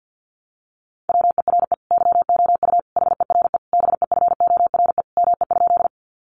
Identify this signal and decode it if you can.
Morse code, heard as a keyed tone, about 48 wpm, 700 Hz. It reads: WERE YOU HERE BEFORE ME?
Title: Sentences from Top 200 Words 50wpm